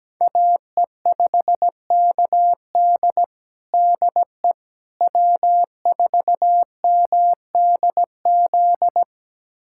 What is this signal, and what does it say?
Morse code, heard as a keyed tone, about 17 wpm, 700 Hz. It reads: AE5KD DE W4MDZ
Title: USA Callsign DE Callsign - 5 Characters - 17wpm